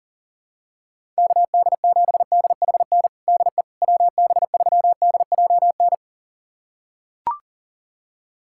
Morse code, heard as a keyed tone, about 40 wpm, 700 Hz. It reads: KD7DHN DE WB3DJN
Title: USA Callsign DE Callsign 40wpm